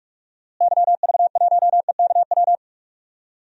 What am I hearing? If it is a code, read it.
Morse code, heard as a keyed tone, about 41 wpm, 700 Hz. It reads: YV1EKW
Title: DXCC-100 Call Signs - 6 Characters - Repeated 3x - 45wpm